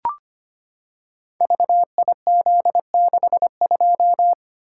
Morse code, heard as a keyed tone, about 25 wpm, 700 Hz. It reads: VIZ62
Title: Code Groups - 5 Letter:Numbers - 25wpm